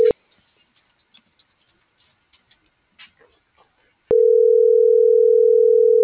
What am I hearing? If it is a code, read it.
Ringback tone (North America), 440 + 480 Hz.